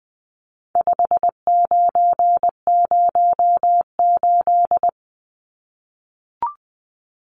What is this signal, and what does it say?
Morse code, heard as a keyed tone, about 20 wpm, 700 Hz. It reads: 5908
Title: Numbers - 4-Digit 20wpm